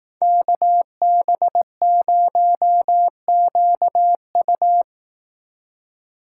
Morse code, heard as a keyed tone, about 18 wpm, 700 Hz. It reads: KB0QU
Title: Speed Racing - USA Call Signs 15wpm